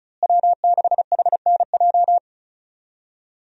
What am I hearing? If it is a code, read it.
Morse code, heard as a keyed tone, about 36 wpm, 700 Hz. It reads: W6HNJ
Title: USA Call Signs - 5 Characters - Repeated 3x - 35wpm